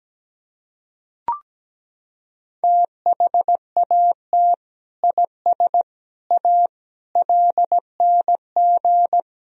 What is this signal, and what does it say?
Morse code, heard as a keyed tone, about 17 wpm, 700 Hz. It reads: THAT IS A LNG
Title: Sentences from Top 100 Words and Abbreviations 17wpm